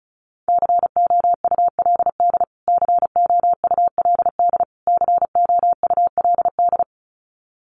Morse code, heard as a keyed tone, about 34 wpm, 700 Hz. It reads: COULD COULD COULD
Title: Top 200 Words Repeated 35wpm